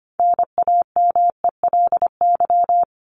Morse code, heard as a keyed tone, about 25 wpm, 700 Hz. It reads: NAMELY
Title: LY Suffix - Top 100 Words with Base Words 25wpm